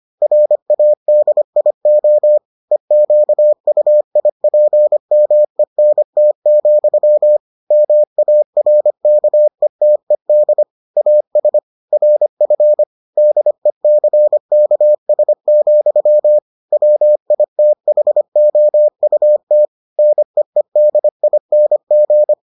Morse code, heard as a keyed tone, about 25 wpm, 600 Hz. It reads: RADIO EQUIPMENT, MARKETED AS RF DECKS, WITHOUT NEEDING